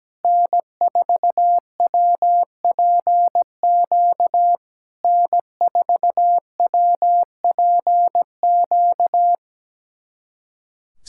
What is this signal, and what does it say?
Morse code, heard as a keyed tone, about 17 wpm, 700 Hz. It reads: N4WPQ N4WPQ
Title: SOTA QSO Practice 17wpm